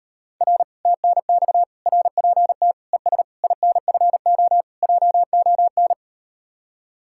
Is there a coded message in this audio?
Yes: Morse code from 0.4 to 5.9 s, heard as a keyed tone, about 38 wpm, 700 Hz. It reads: R TNX RPT ES INFO JON